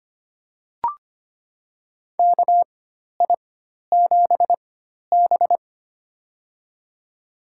Morse code, heard as a keyed tone, about 25 wpm, spaced as for 12 wpm, 700 Hz. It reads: KI7B
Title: USA Call Signs - 4 Characters - 4x Character Spacing - 25wpm